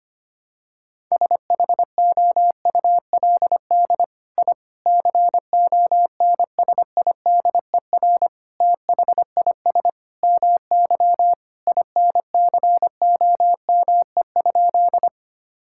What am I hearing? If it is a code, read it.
Morse code, heard as a keyed tone, about 25 wpm, 700 Hz. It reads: SHOULD I CONSIDER THIS MY INCOME?
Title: Sentences from Top 1000 Words - Review Part 5 - 25wpm